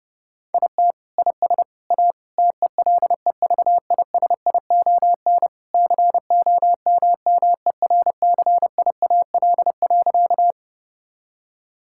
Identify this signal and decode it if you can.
Morse code, heard as a keyed tone, about 30 wpm, 700 Hz. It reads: IT IS A TELEVISION COMMERCIAL.
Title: Sentences from Top 1000 Words - Review Part 5 - 30wpm